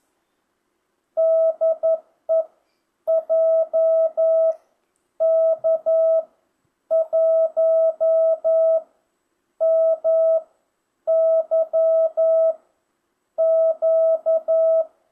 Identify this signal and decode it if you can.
Morse code, heard as a keyed tone, about 11 wpm, spaced as for 8 wpm, 650 Hz. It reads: DE JK1MYQ
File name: jh8cbh / 2025cw